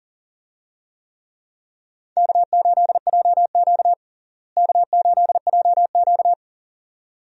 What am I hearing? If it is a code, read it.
Morse code, heard as a keyed tone, about 40 wpm, 700 Hz. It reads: K8JQ K8JQ
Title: USA Call Signs - 4 Characters - Repeated 2x - 40wpm